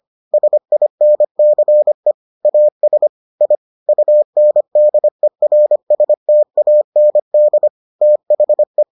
Morse code, heard as a keyed tone, about 25 wpm, 600 Hz. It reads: SINCE AS I UNDERSTAND THE